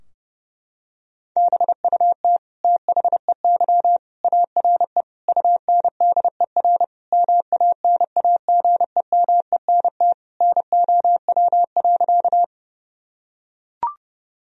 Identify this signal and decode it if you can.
Morse code, heard as a keyed tone, about 32 wpm, 700 Hz. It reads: BUT THEY ARE UNDER MANAGEMENT NOW.
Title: Sentences from Top 1000 Words - Review Part 4 - 30wpm